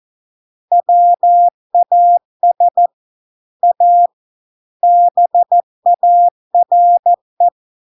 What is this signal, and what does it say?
Morse code, heard as a keyed tone, about 14 wpm, 700 Hz. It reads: WAS A BARE